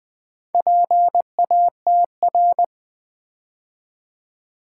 Morse code, heard as a keyed tone, about 20 wpm, 700 Hz. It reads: PATR
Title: Build A Word - 2 Letters At A Time - 6-Letter Words - 20wpm